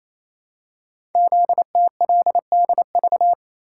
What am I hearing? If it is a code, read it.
Morse code, heard as a keyed tone, about 28 wpm, 700 Hz. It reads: ZTLDV
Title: Code Groups - 5 Letters - 28wpm